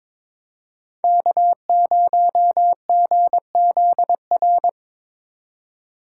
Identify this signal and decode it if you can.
Morse code, heard as a keyed tone, about 26 wpm, 700 Hz. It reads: K0GZR
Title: Speed Racing - USA 1x3 Call Signs 22wpm